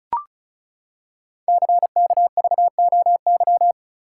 Morse code, heard as a keyed tone, about 36 wpm, 690 Hz. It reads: CKVOY
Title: Code Groups - 5 Letters - 35wpm